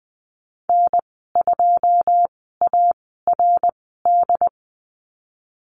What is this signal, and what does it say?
Morse code, heard as a keyed tone, about 20 wpm, spaced as for 14 wpm, 700 Hz. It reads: N2ARD
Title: 082 Lesson 14g - Number 2 - Call Signs 20wpm